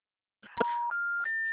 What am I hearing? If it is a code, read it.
Special information tone.
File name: out- -2005